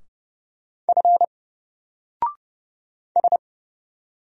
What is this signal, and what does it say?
Morse code, heard as a keyed tone, about 30 wpm, 700 Hz. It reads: F S
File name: Single-Letter-Number-30wpm